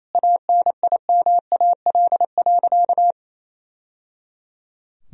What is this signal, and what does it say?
Morse code, heard as a keyed tone, about 28 wpm, 700 Hz. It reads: ANIMAL.
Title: Sentences from Top 1000 Words - Review Part 1 - 28wpm